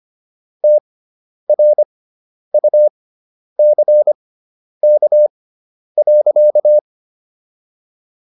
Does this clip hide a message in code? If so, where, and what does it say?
Yes: Morse code from 0.6 to 6.8 s, heard as a keyed tone, about 25 wpm, spaced as for 10 wpm, 600 Hz. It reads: TRUCK.